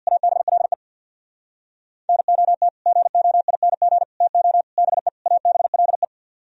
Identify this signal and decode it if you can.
Morse code, heard as a keyed tone, about 66 wpm, 700 Hz. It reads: ABLE NOT GOING TO BE ABLE